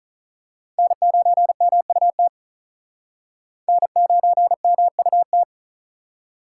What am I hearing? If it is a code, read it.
Morse code, heard as a keyed tone, about 41 wpm, 700 Hz. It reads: N9MUT N9MUT
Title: Speed Racing - USA 1x3 Call Signs 35wpm